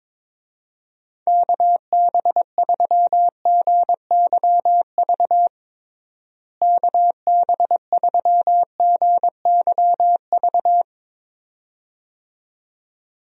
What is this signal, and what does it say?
Morse code, heard as a keyed tone, about 22 wpm, 700 Hz. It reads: KB3GYV KB3GYV
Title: USA Call Signs - 6 Characters - Repeated 2x - 22wpm